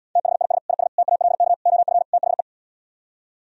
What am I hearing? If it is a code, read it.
Morse code, heard as a keyed tone, about 78 wpm, 700 Hz. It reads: THIS IS TALK OF THE